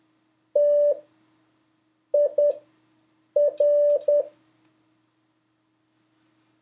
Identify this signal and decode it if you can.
Morse code, heard as a keyed tone, about 10 wpm, 580 Hz. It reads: T I R
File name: 3of5-words-2